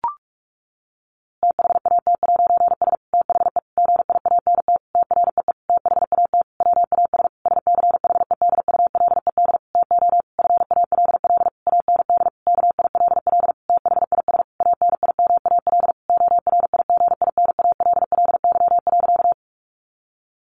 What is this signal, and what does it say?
Morse code, heard as a keyed tone, about 44 wpm, 700 Hz. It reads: THAT'S THE GIANT TREE THAT WAS SCHEDULED TO FALL AND KILL THIS ANIMAL ORIGINALLY.
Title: A Sound Of Thunder 45wpm